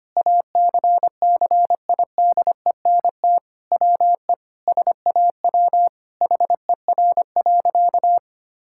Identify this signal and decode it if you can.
Morse code, heard as a keyed tone, about 25 wpm, 700 Hz. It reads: ACCIDENT WE SAW HER.
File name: practice / 25wpm / Sentences with 4-Word Phrases - 25wpm